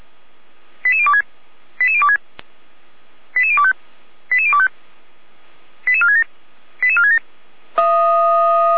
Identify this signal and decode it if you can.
Five-tone selective call (ZVEI), heard as quick runs of five tones, tones of about 70 ms. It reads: 88026 88026 88025 88025 88467 88467